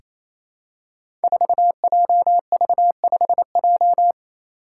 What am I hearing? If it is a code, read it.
Morse code, heard as a keyed tone, about 28 wpm, 700 Hz. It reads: 4JV5J
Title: Five Challenging Letter-Number Combinations 28wpm